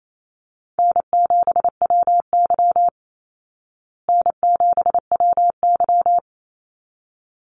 Morse code, heard as a keyed tone, about 28 wpm, 700 Hz. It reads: N7WY N7WY
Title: Call Signs Repeated 3x - 28wpm